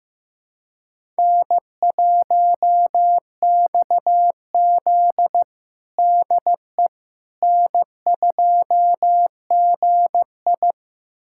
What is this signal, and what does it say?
Morse code, heard as a keyed tone, about 15 wpm, 700 Hz. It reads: N1XZ DE N2GI
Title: USA Callsign DE Callsign - 4 Characters - 15wpm